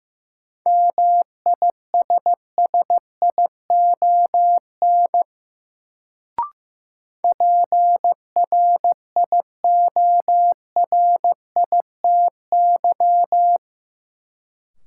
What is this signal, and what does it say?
Morse code, heard as a keyed tone, about 16 wpm, 700 Hz. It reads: MISSION PRIORITY
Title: Sentences from Top 1600 Words - New Word Drill Alt - 15wpm